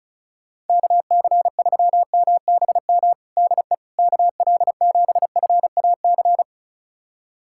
Morse code, heard as a keyed tone, about 36 wpm, 700 Hz. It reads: KC3MBM DE KL7FAC
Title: USA Callsign DE Callsign 35wpm